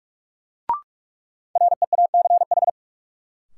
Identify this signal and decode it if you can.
Morse code, heard as a keyed tone, about 46 wpm, 700 Hz. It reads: REACH
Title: Sentences from Top 400 Words - New Word Drill - 45wpm